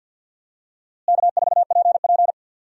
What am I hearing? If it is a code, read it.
Morse code, heard as a keyed tone, about 46 wpm, 700 Hz. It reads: K4PP
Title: USA Call Signs - 4 Characters - Repeated 3x - 50wpm